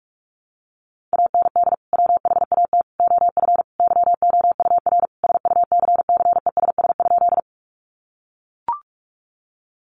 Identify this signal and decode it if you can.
Morse code, heard as a keyed tone, about 44 wpm, 700 Hz. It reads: AND WHAT OF YOUR SUCCESS?
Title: Sentences from Top 900 Words 45wpm